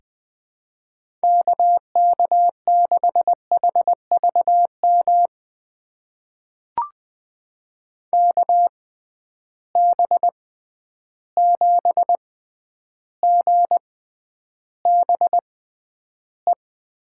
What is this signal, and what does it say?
Morse code, heard as a keyed tone, about 20 wpm, spaced as for 10 wpm, 700 Hz. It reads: KK6HVM KB7GBE
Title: USA Call Signs - 6 Characters - 6x Character Spacing - 20wpm